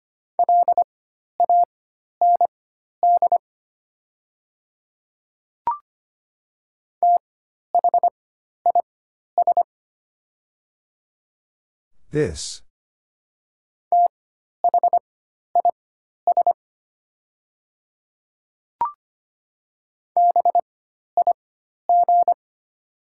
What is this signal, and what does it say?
Morse code, heard as a keyed tone, about 25 wpm, spaced as for 12 wpm, 700 Hz. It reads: LAND THIS THIS BIG
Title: 4x Character Spacing - Top 200 Words - 25wpm